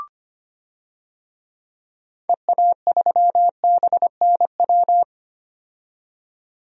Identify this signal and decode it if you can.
Morse code, heard as a keyed tone, about 25 wpm, 700 Hz. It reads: EA3BNW
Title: DXCC-100 Call Signs - 6 Characters - 25wpm